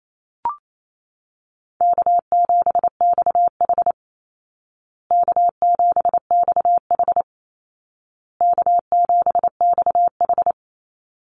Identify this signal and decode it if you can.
Morse code, heard as a keyed tone, about 28 wpm, 700 Hz. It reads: K7XH K7XH K7XH
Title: Call Signs Repeated 2x - 28wpm